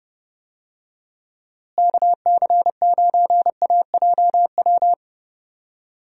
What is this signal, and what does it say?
Morse code, heard as a keyed tone, about 30 wpm, 700 Hz. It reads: KC9AJW
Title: USA Call Signs - 6 Characters - Repeated 3x - 30wpm